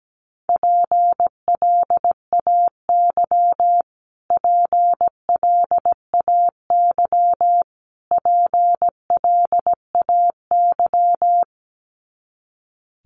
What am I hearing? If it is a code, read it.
Morse code, heard as a keyed tone, about 17 wpm, 700 Hz. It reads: PLAY PLAY PLAY
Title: Top 500 Words Repeated 17wpm